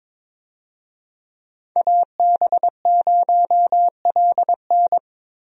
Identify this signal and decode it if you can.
Morse code, heard as a keyed tone, about 21 wpm, 700 Hz. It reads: AB0LN
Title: USA Call Signs - 5 Characters - 22wpm